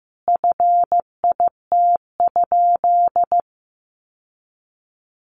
Morse code, heard as a keyed tone, about 14 wpm, 700 Hz. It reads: FIT?
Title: Sentences from Top 800 Words 15wpm